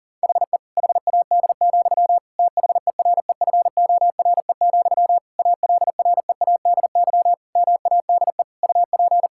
Morse code, heard as a keyed tone, about 40 wpm, 700 Hz. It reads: HE HAD, THEREFORE, ALREADY MADE UP
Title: Sentences from Top 1000 Words - Review Part 5 - 40wpm